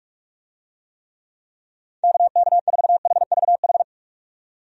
Morse code, heard as a keyed tone, about 46 wpm, 700 Hz. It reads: KK4HVH
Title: USA Call Signs - 6 Characters - 45wpm